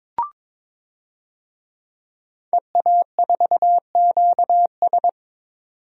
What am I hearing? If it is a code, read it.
Morse code, heard as a keyed tone, about 22 wpm, 700 Hz. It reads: EA4QS
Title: DXCC-100 Call Signs - 5 Characters - 22wpm